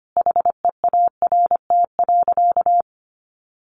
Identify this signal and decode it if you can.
Morse code, heard as a keyed tone, about 25 wpm, 700 Hz. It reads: HEART.
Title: Sentences from Top 1400 Words 25wpm